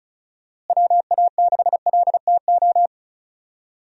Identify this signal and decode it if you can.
Morse code, heard as a keyed tone, about 36 wpm, 700 Hz. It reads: WA6LTO
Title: USA Call Signs - 6 Characters - Repeated 2x - 35wpm